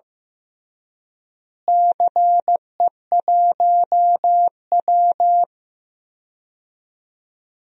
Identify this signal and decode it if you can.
Morse code, heard as a keyed tone, about 15 wpm, 700 Hz. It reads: CE1W
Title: DXCC-100 Call Signs - 4 Characters - Repeated 2x - 15wpm